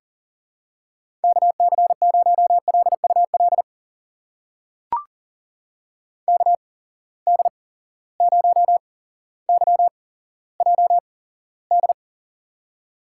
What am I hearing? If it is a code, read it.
Morse code, heard as a keyed tone, about 42 wpm, spaced as for 17 wpm, 700 Hz. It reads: KC0LUL KD0YJD
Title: USA Call Signs - 6 Characters - 8x Character Spacing - 40wpm